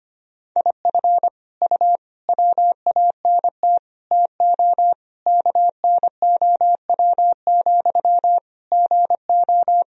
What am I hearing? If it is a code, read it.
Morse code, heard as a keyed tone, about 25 wpm, 700 Hz. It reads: IF U WANT TO KNOW, GO
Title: Sentences from Top 200 Words and Abbreviations 25wpm